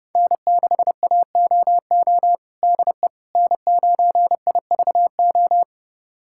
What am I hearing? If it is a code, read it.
Morse code, heard as a keyed tone, about 31 wpm, 700 Hz. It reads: N6AOO DE N9IVO
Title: USA Callsign DE Callsign - 5 Characters - 30wpm